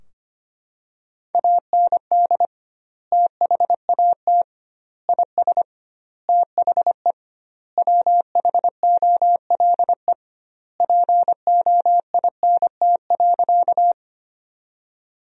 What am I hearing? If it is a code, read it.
Morse code, heard as a keyed tone, about 25 wpm, 700 Hz. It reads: AND THAT IS THE WHOLE POINT.
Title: 2x Word-Spacing - Sentences from Top 400 Words - 25wpm